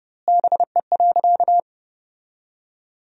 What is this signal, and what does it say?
Morse code, heard as a keyed tone, about 32 wpm, 700 Hz. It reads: BE.
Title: 2x Word-Spacing - Sentences from Top 200 Words - 30wpm